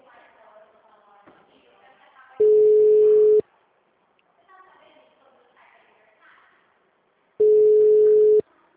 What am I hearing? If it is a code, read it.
Ringback tone (Europe, 425 Hz), with speech.
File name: out- -2016- - - .6977